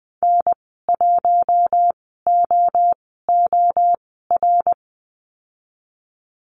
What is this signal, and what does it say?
Morse code, heard as a keyed tone, about 20 wpm, spaced as for 14 wpm, 700 Hz. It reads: N1OOR
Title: 069 Lesson 12g - Letter d - Call Signs 20wpm